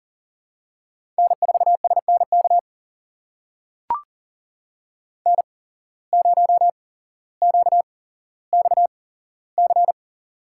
Morse code, heard as a keyed tone, about 40 wpm, spaced as for 17 wpm, 700 Hz. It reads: N4SNK N0QXC
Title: USA Call Signs - 5 Characters - 8x Character Spacing - 40wpm